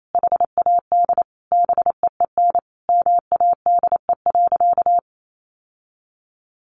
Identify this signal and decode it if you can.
Morse code, heard as a keyed tone, about 28 wpm, 700 Hz. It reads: HAD BEEN MADE.
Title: Sentences from Top 1400 Words 28wpm